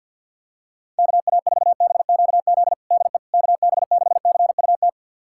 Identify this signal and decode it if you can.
Morse code, heard as a keyed tone, about 52 wpm, 700 Hz. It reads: KA4BQZ DE KB6CUT
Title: USA Callsign DE Callsign - 6 Characters - 50wpm